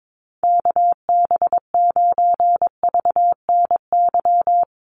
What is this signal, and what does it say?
Morse code, heard as a keyed tone, about 22 wpm, 700 Hz. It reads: KB9VNY
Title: USA Call Signs Repeated 3x 22wpm